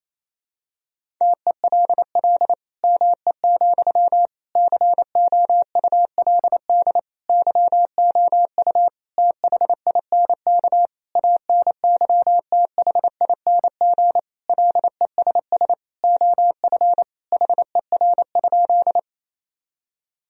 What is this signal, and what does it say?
Morse code, heard as a keyed tone, about 27 wpm, 700 Hz. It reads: TELL ME, COULD YOU THINK ANYTHING LESS OF HER?
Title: Sentences from Top 1000 Words - Review Part 2 - 28wpm